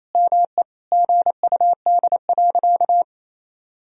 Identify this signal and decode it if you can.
Morse code, heard as a keyed tone, about 28 wpm, 700 Hz. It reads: ME GUD.
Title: Sentences from Top 500 Words and Abbreviations 28wpm